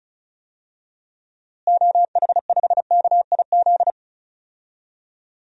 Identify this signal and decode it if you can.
Morse code, heard as a keyed tone, about 34 wpm, 700 Hz. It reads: OH5KIZ
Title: DXCC-100 Call Signs - 6 Characters - Repeated 3x - 35wpm